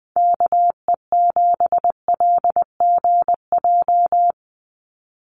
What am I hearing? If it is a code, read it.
Morse code, heard as a keyed tone, about 20 wpm, 700 Hz. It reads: KE7LGJ
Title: USA Call Signs 20wpm